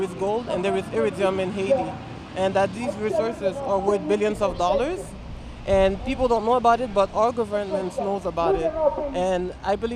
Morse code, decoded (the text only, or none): none